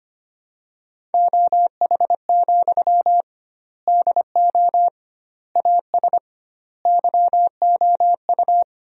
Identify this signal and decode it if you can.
Morse code, heard as a keyed tone, about 25 wpm, 700 Hz. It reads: OH, DO AS YOU
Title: 2x Word-Spacing - Sentences from Top 500 Words - 25wpm